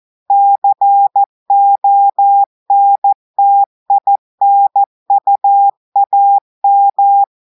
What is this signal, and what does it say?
Morse code, heard as a keyed tone, about 14 wpm, 800 Hz. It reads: CONTINUAM
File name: audios / portugais / portugais texte en clair 2 vitesse 14 mots minutes